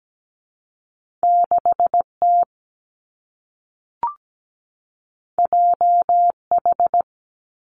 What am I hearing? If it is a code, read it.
Morse code, heard as a keyed tone, about 17 wpm, 700 Hz. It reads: 6 T J H
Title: Two Challenging Letter-Number Combinations 17wpm